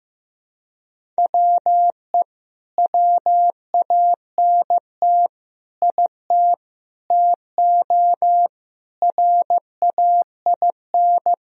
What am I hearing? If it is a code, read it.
Morse code, heard as a keyed tone, about 15 wpm, 700 Hz. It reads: WE WANT IT TO RAIN
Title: Sentences from Top 1000 Words - Review Part 1 - 15wpm